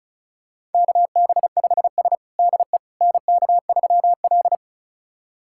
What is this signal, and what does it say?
Morse code, heard as a keyed tone, about 34 wpm, 700 Hz. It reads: KB5S DE NK3L
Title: USA Callsign DE Callsign - 4 Characters - 35wpm